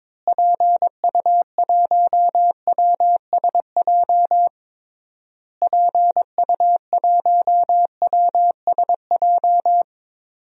Morse code, heard as a keyed tone, about 22 wpm, 700 Hz. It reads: PU1WSJ PU1WSJ
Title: DXCC-100 Call Signs - 6 Characters - Repeated 3x - 22wpm